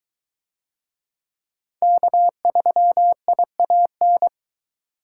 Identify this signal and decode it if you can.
Morse code, heard as a keyed tone, about 20 wpm, 700 Hz. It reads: K3IAN